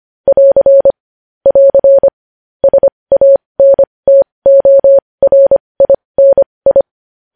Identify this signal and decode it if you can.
Morse code, heard as a keyed tone, about 25 wpm, spaced as for 20 wpm, 550 Hz. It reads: <AR> <AR> SANTORINI
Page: News Headlines in Morse Code at 20 WPM